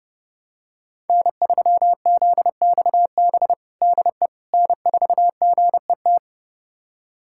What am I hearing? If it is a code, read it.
Morse code, heard as a keyed tone, about 32 wpm, 700 Hz. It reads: N3ZXB DE N4GET
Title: USA Callsign DE Callsign - 5 Characters - 30wpm